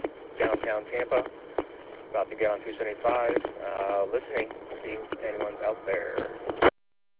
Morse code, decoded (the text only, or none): none